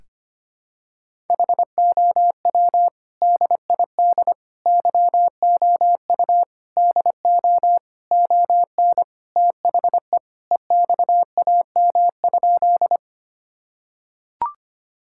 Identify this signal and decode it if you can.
Morse code, heard as a keyed tone, about 25 wpm, 700 Hz. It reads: HOW DID YOU DO ON THE EXAM?
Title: Sentences from Top 1000 Words - Review Part 5 - 25wpm